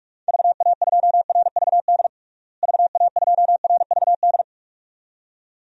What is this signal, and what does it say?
Morse code, heard as a keyed tone, about 46 wpm, 700 Hz. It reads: VA2RVD VA2RVD
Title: DXCC-100 Call Signs - 6 Characters - Repeated 2x - 45wpm